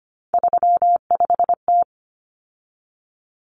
Morse code, heard as a keyed tone, about 25 wpm, 700 Hz. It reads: 35T
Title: Common Cut Numbers - 3-Digit 25wpm